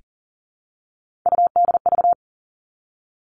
Morse code, heard as a keyed tone, about 40 wpm, 700 Hz. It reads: UDV
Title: All Cut Numbers - 3-Digit 40wpm